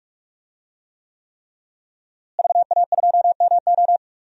Morse code, heard as a keyed tone, about 46 wpm, 700 Hz. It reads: VA2MO
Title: DXCC-100 Call Signs - 5 Characters - Repeated 3x - 45wpm